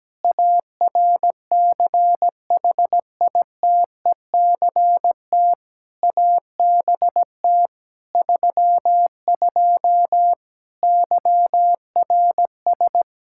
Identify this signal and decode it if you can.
Morse code, heard as a keyed tone, about 17 wpm, 700 Hz. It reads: ARCHITECT ABT 32 YRS
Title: Practice QSO - Exchange 3 - More Info - 17wpm